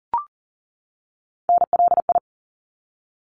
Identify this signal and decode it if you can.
Morse code, heard as a keyed tone, about 40 wpm, 700 Hz. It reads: NLI
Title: ARRL Sections 40wpm